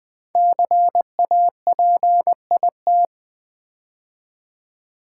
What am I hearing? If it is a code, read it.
Morse code, heard as a keyed tone, about 20 wpm, 700 Hz. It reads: CAPIT
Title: Build A Word - 5 Letters At A Time - 10-Letter Words - 20wpm